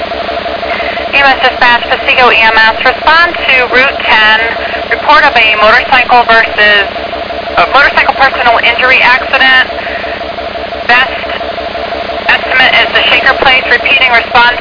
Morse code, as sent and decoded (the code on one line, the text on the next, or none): none